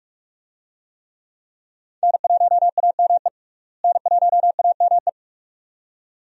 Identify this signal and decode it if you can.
Morse code, heard as a keyed tone, about 47 wpm, 700 Hz. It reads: N1AME N1AME
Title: USA Call Signs - 5 Characters - Repeated 2x - 45wpm